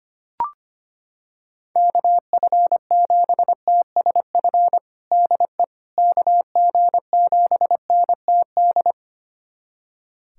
Morse code, heard as a keyed tone, about 25 wpm, 700 Hz. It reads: KF7TSF DE KG7NTD
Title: USA Callsign DE Callsign - 6 Characters - 25wpm